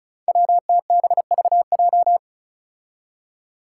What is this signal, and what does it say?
Morse code, heard as a keyed tone, about 34 wpm, 700 Hz. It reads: WTBVJ
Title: Five Challenging Letter-Number Combinations 35wpm